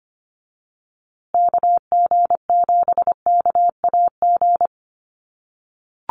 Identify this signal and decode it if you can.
Morse code, heard as a keyed tone, about 25 wpm, 700 Hz. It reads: KG7KAG
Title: USA Call Signs Repeated 3x 25wpm